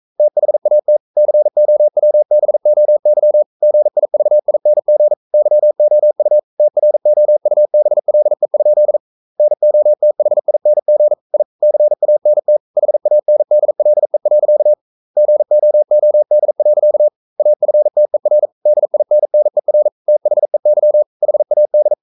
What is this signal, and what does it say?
Morse code, heard as a keyed tone, about 42 wpm, 600 Hz. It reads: THAT COWBOY GIVING YOU TROUBLE? NOTHING I CANT HANDLE. GOOD. AFTER DINNER THEY HAD
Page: Romanen Never af Ken Follet på engelsk.